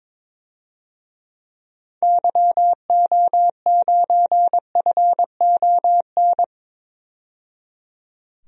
Morse code, heard as a keyed tone, about 22 wpm, 700 Hz. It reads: YO9FON
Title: DXCC-100 Call Signs - 6 Characters - 22wpm